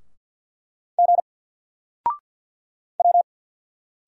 Morse code, heard as a keyed tone, about 54 wpm, 700 Hz. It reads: G W
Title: Single Letter-Number Alt 50wpm